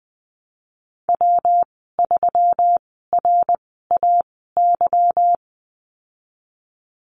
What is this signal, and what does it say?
Morse code, heard as a keyed tone, about 20 wpm, spaced as for 15 wpm, 700 Hz. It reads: W3RAY
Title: 167 Lesson 27g - Number 7 - Call Signs 20wpm